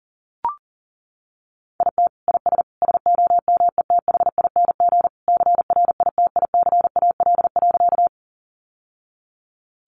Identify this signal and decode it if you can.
Morse code, heard as a keyed tone, about 40 wpm, 700 Hz. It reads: IT IS SOMETHING CRITICAL.
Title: Sentences from Top 1100 Words 40wpm